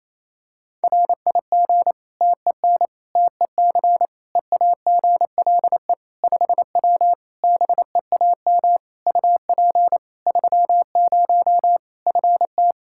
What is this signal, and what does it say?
Morse code, heard as a keyed tone, about 28 wpm, 700 Hz. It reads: RIG TEN TEC EAGLE 5W BEAM UP 30 FT
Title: Practice QSO - Exchange 2 - Info - 28wpm